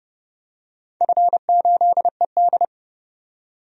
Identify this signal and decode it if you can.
Morse code, heard as a keyed tone, about 30 wpm, 700 Hz. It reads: F8ED
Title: DXCC-100 Call Signs - 4 Characters - 30wpm